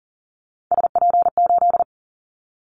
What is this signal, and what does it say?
Morse code, heard as a keyed tone, about 40 wpm, 700 Hz. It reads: SP8
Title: Callsign Prefixes for Top 25 DXCC - Repeated 3x - 40wpm